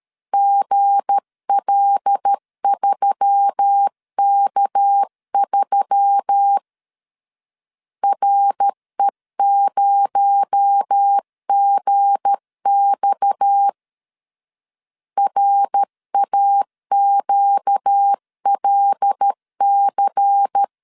Morse code, heard as a keyed tone, about 13 wpm, 800 Hz. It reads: GL3K3 RE0GX RAQLC
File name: LEZIONI / 7 GF9Z / 7 G 12wpm 2055